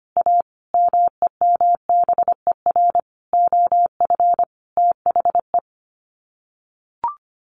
Morse code, heard as a keyed tone, about 25 wpm, 700 Hz. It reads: A MEMBER OF THE
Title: Sets of 5 Words 25wpm